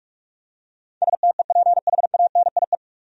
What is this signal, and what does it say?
Morse code, heard as a keyed tone, about 45 wpm, 700 Hz. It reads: STEPHANIE
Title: USA Common Names - 45wpm